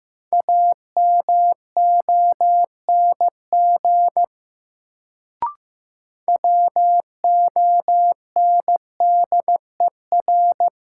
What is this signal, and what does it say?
Morse code, heard as a keyed tone, about 15 wpm, 700 Hz. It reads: AMONG WONDER
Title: Sentences from Top 500 Words - New Word Drill - 15wpm